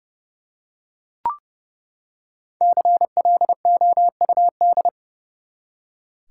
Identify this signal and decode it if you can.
Morse code, heard as a keyed tone, about 32 wpm, 700 Hz. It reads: CLOUD
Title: Five Letter Words - 30wpm